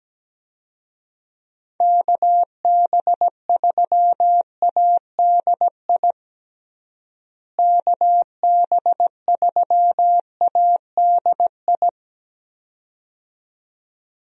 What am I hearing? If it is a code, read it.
Morse code, heard as a keyed tone, about 17 wpm, 700 Hz. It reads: KB3ADI KB3ADI
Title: USA Call Signs - 6 Characters - Repeated 2x - 17wpm